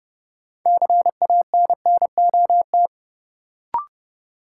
Morse code, heard as a keyed tone, about 30 wpm, 700 Hz. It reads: CANNOT
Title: Sentences from Top 1000 Words - Review Part 6 - New Word Drill - 30wpm